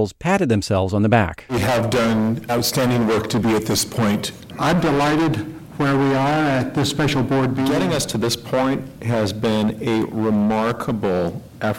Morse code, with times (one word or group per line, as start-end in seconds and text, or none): none